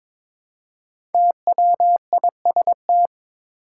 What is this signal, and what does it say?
Morse code, heard as a keyed tone, about 22 wpm, 700 Hz. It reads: TWIST
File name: 5 Letter Words - First 3 Letters Given - 22wpm